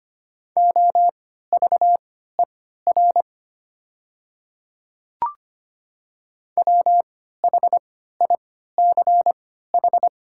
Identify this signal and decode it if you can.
Morse code, heard as a keyed tone, about 25 wpm, spaced as for 14 wpm, 700 Hz. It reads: OVER WHICH
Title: 3x Character Spacing - Top 100 Words - 25wpm